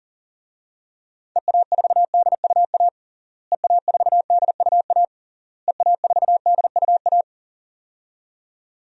Morse code, heard as a keyed tone, about 40 wpm, 700 Hz. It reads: EA4DUA EA4DUA EA4DUA
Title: DXCC-100 Call Signs - 6 Characters - Repeated 3x - 40wpm